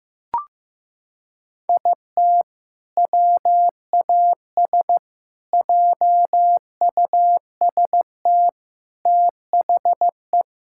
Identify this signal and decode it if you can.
Morse code, heard as a keyed tone, about 15 wpm, 700 Hz. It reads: IT WAS JUST THE
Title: Sentences from Top 1000 Words - Review Part 5 - 15wpm